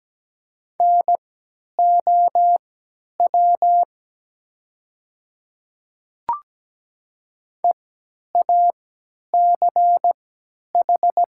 Morse code, heard as a keyed tone, about 17 wpm, spaced as for 10 wpm, 700 Hz. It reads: NOW EACH
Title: 3x Character Spacing - Top 100 Words - 17wpm